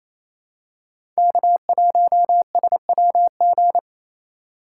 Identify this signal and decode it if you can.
Morse code, heard as a keyed tone, about 28 wpm, 700 Hz. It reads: K1SWG
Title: USA Call Signs - 5 Characters - 4x Character Spacing - 28wpm